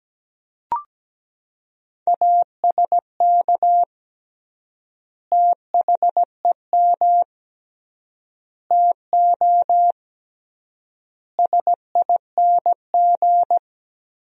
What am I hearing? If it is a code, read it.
Morse code, heard as a keyed tone, about 17 wpm, 700 Hz. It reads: ASK THEM TO SING
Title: 3x Word-Spacing - Sentences from Top 400 Words - 17wpm